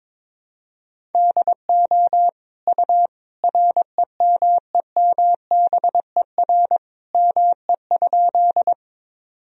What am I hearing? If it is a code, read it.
Morse code, heard as a keyed tone, about 22 wpm, 700 Hz. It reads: DO U REMEMBER ME?
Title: Sentences from Top 400 Words and Abbreviations 22wpm